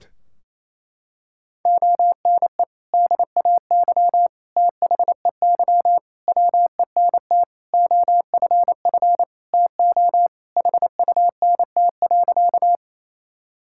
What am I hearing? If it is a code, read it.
Morse code, heard as a keyed tone, about 28 wpm, 700 Hz. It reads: ONE DAY THEY WENT OFF TO HUNT.
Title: Sentences from Top 1600 Words 28wpm